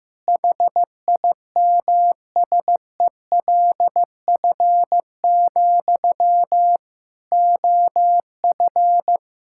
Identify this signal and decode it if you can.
Morse code, heard as a keyed tone, about 15 wpm, 700 Hz. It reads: HIMSELF, OF
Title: Sentences from Top 1600 Words 15wpm